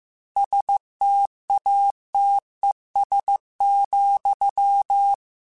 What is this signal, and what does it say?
Morse code, heard as a keyed tone, about 15 wpm, 800 Hz. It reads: STATES,
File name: Technical Subjects / Technical Subjects-mp3s / AND Gate @13